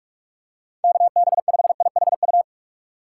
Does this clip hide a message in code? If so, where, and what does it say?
Yes: Morse code from 0.8 to 2.4 s, heard as a keyed tone, about 42 wpm, 700 Hz. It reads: KB5IHU